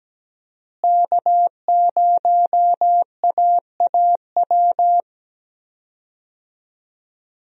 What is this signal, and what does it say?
Morse code, heard as a keyed tone, about 17 wpm, 700 Hz. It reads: K0AAW
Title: USA Call Signs - 5 Characters - Repeated 2x - 17wpm